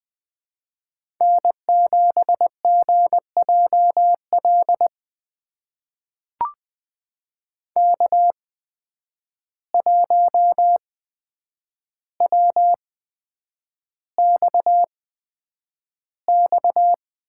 Morse code, heard as a keyed tone, about 20 wpm, spaced as for 9 wpm, 700 Hz. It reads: N7GJL K1WXX
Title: USA Call Signs - 5 Characters - 8x Character Spacing - 20wpm